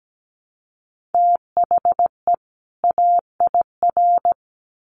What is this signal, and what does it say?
Morse code, heard as a keyed tone, about 17 wpm, 700 Hz. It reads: THE AIR
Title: Sets of 2 Words 17wpm